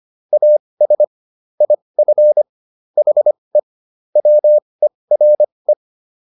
Morse code, heard as a keyed tone, about 25 wpm, spaced as for 20 wpm, 600 Hz. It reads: AS IF HE WERE